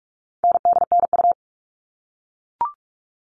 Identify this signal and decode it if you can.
Morse code, heard as a keyed tone, about 44 wpm, 700 Hz. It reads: NDNU
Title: All Cut Numbers - 4-Digit 45wpm